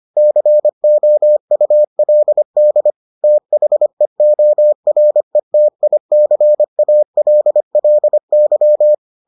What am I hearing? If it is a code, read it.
Morse code, heard as a keyed tone, about 25 wpm, 600 Hz. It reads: COULD THEORETICALLY